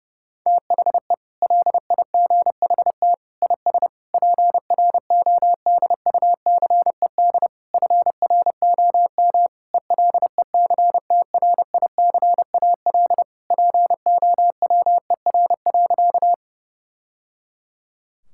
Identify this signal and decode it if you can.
Morse code, heard as a keyed tone, about 32 wpm, 700 Hz. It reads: THE LIGHT IS PRODUCED FROM ELECTRICAL POWER.
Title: Sentences from Top 1000 Words - Review Part 6 - 30wpm